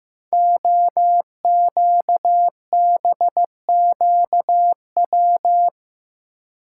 Morse code, heard as a keyed tone, about 15 wpm, 700 Hz. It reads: OQBQW
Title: Code Groups - 5 Letters - 15wpm